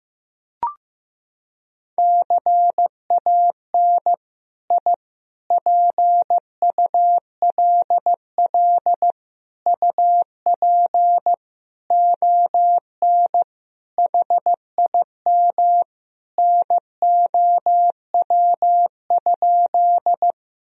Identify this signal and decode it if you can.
Morse code, heard as a keyed tone, about 15 wpm, 700 Hz. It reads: CAN I PULL UP ON HIM NOW?
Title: Sentences from Top 1000 Words - Review Part 2 - 15wpm